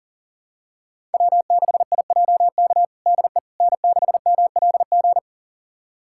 Morse code, heard as a keyed tone, about 40 wpm, 700 Hz. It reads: W6IJK DE N6MLG
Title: USA Callsign DE Callsign - 5 Characters - 40wpm